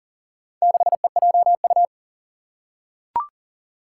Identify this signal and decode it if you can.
Morse code, heard as a keyed tone, about 40 wpm, 700 Hz. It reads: 6EJU
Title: Four Challenging Letter-Number Combinations 40wpm